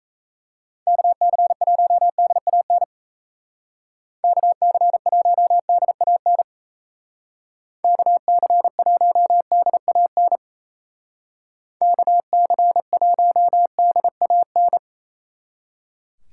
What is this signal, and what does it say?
Morse code, heard as a keyed tone, about 35 wpm, 700 Hz. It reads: KC1DAN KC1DAN KC1DAN KC1DAN